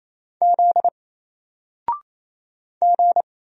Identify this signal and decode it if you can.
Morse code, heard as a keyed tone, about 28 wpm, 700 Hz. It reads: Z G
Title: Single Character (Letter:Number:Punctuation:Prosign) 28wpm